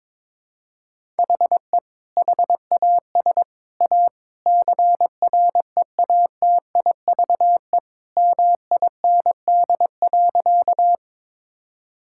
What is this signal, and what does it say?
Morse code, heard as a keyed tone, about 22 wpm, 700 Hz. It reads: HE HAS A CREATIVE MIND.
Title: Sentences from Top 1000 Words - Review Part 5 - 22wpm